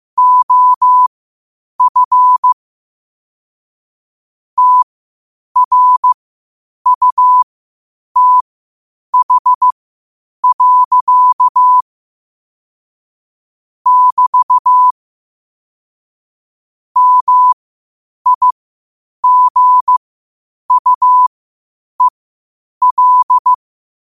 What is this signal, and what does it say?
Morse code, heard as a keyed tone, about 15 wpm, spaced as for 8 wpm, 1000 Hz. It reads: OF TRUTH. <BT> MIGUEL
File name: qotdfiles / 8wpm